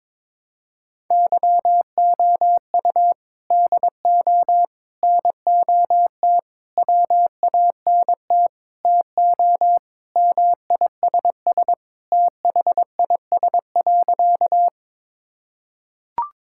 Morse code, heard as a keyed tone, about 22 wpm, 700 Hz. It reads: YOU DO NOT WANT TO MISS THIS.
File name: practice / 22wpm / Sentences with 4-Word Phrases - 22wpm